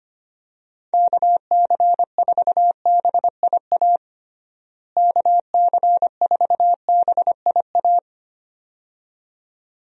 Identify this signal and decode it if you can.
Morse code, heard as a keyed tone, about 25 wpm, 700 Hz. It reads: KC4BIA KC4BIA
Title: USA Call Signs - 6 Characters - Repeated 2x - 25wpm